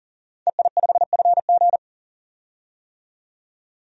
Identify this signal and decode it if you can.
Morse code, heard as a keyed tone, about 39 wpm, 700 Hz. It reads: EI5FG
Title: DXCC-100 Call Signs - 5 Characters - 40wpm